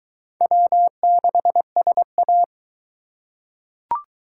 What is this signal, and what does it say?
Morse code, heard as a keyed tone, about 22 wpm, 700 Hz. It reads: W6SA